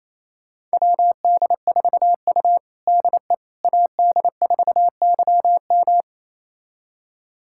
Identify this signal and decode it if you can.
Morse code, heard as a keyed tone, about 28 wpm, 700 Hz. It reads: WD4U DE AD4YM
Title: USA Callsign DE Callsign 28wpm